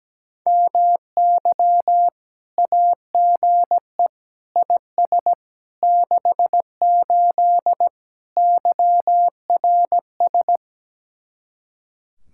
Morse code, heard as a keyed tone, about 17 wpm, 700 Hz. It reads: MY AGE IS 68 YRS
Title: Speed-Racing Common QSO Phrases - 17wpm